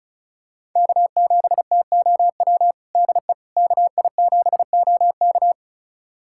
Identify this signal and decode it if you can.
Morse code, heard as a keyed tone, about 36 wpm, 700 Hz. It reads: K7TOW DE KI7OK
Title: USA Callsign DE Callsign - 5 Characters - 35wpm